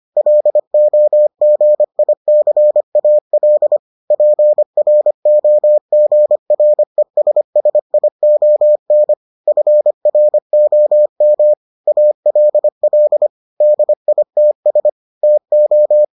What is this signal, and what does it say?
Morse code, heard as a keyed tone, about 25 wpm, 600 Hz. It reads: LOGICAL PROGRESSION FROM ALL DITS TO